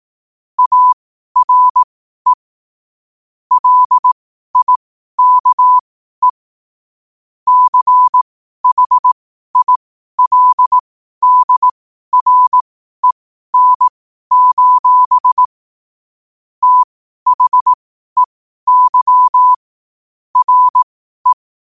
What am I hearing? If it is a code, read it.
Morse code, heard as a keyed tone, about 18 wpm, spaced as for 12 wpm, 1000 Hz. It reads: ARE LIKE CHILDREN: THEY RE
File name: qotdfiles / 12wpm